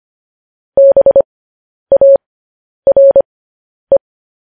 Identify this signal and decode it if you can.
Morse code, heard as a keyed tone, about 25 wpm, spaced as for 10 wpm, 550 Hz. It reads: BARE
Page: news-headlines-in-morse-code-at-10-wpm